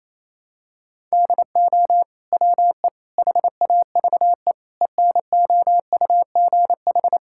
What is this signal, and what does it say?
Morse code, heard as a keyed tone, about 28 wpm, 700 Hz. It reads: DO WE HAVE ENOUGH
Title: Sentences from Top 1000 Words - Review Part 5 - 28wpm